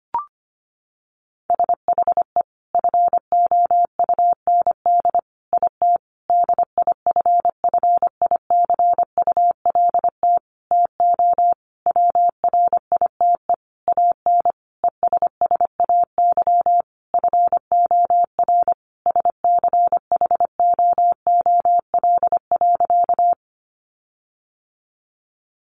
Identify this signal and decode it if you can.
Morse code, heard as a keyed tone, about 25 wpm, 700 Hz. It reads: SHE FOUND IT DIFFICULT TO WRITE AN ESSAY FOR SCHOOL.
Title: Sentences from Top 1500 Words 25wpm